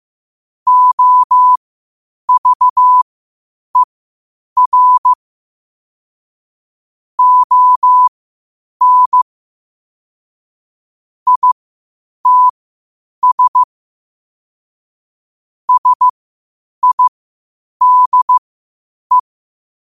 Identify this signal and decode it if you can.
Morse code, heard as a keyed tone, about 15 wpm, spaced as for 8 wpm, 1000 Hz. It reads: OVER ON ITS SIDE
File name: qotdfiles / 8wpm